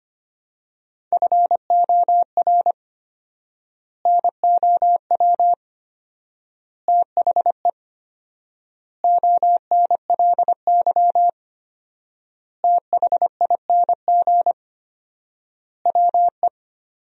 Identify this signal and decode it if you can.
Morse code, heard as a keyed tone, about 25 wpm, 700 Hz. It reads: FOR NOW THE ONLY THING WE
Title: 4x Word-Spacing - Sentences from Top 200 Words - 25wpm